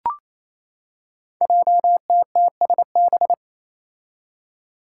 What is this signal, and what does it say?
Morse code, heard as a keyed tone, about 28 wpm, 700 Hz. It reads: JTTSB
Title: Five Challenging Letter-Number Combinations 28wpm